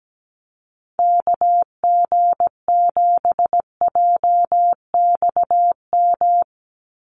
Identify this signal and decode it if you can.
Morse code, heard as a keyed tone, about 17 wpm, 700 Hz. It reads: KG7JXM
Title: USA Call Signs 17wpm